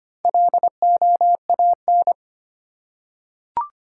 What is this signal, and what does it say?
Morse code, heard as a keyed tone, about 25 wpm, 700 Hz. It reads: LOAN
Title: Four Letter Words - 25wpm